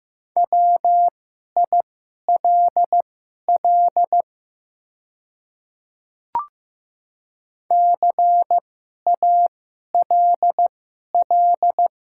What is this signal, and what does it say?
Morse code, heard as a keyed tone, about 15 wpm, spaced as for 11 wpm, 700 Hz. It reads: WILL CALL
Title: 2x Character Spacing - Top 100 Words - 15wpm